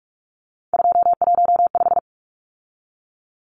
Morse code, heard as a keyed tone, about 44 wpm, 700 Hz. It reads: 215
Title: Numbers - 3-Digit 45wpm